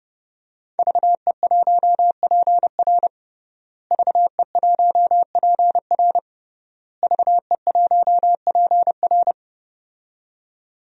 Morse code, heard as a keyed tone, about 30 wpm, 700 Hz. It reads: VE1PR VE1PR VE1PR
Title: DXCC-100 Call Signs - 5 Characters - Repeated 3x - 30wpm